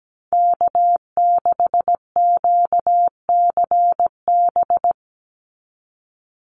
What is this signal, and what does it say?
Morse code, heard as a keyed tone, about 17 wpm, 700 Hz. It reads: K6QCB
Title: NA SOTA Chaser Callsigns 17wpm